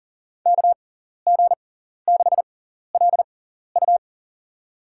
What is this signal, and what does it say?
Morse code, heard as a keyed tone, about 40 wpm, spaced as for 14 wpm, 690 Hz. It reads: KG6LU
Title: USA Call Signs - 5 Characters - 6x Character Spacing - 40wpm